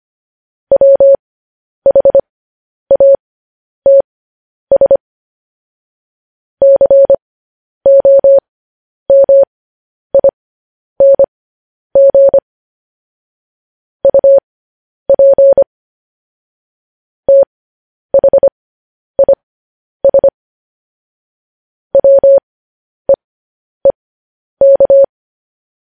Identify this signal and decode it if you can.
Morse code, heard as a keyed tone, about 25 wpm, spaced as for 10 wpm, 550 Hz. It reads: WHATS COMING UP THIS WEEK
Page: news-headlines-in-morse-code-at-10-wpm